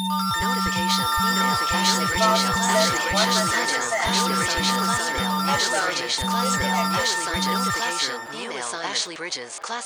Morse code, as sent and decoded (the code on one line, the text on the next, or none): none